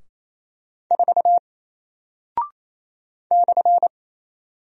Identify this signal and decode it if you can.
Morse code, heard as a keyed tone, about 27 wpm, 700 Hz. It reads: 4 /
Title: Single Character (Letter:Number:Punctuation:Prosign) Alt 28wpm